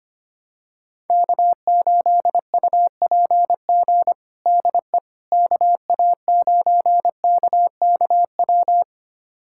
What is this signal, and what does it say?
Morse code, heard as a keyed tone, about 25 wpm, 700 Hz. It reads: K8UPG DE KA9KKW
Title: USA Callsign DE Callsign 25wpm